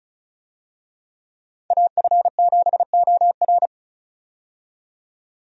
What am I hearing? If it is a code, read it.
Morse code, heard as a keyed tone, about 36 wpm, 700 Hz. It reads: AF7OR
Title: USA Call Signs - 5 Characters - 35wpm